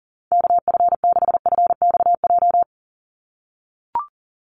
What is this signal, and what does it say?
Morse code, heard as a keyed tone, about 40 wpm, 700 Hz. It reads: KF6FXJ
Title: USA Call Signs 40wpm